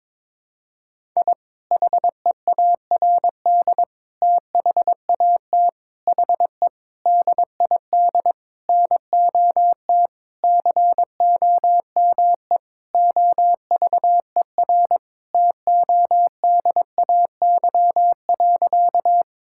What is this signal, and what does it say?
Morse code, heard as a keyed tone, about 22 wpm, 700 Hz. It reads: I HEARD THAT HE DID NOT COME OVER TODAY.
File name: practice / 22wpm / Sentences with 4-Word Phrases - 22wpm